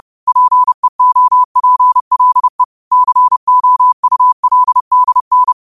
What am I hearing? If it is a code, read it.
Morse code, heard as a keyed tone, about 30 wpm, 1000 Hz. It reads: PEOPLE COULDN
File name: 30wpm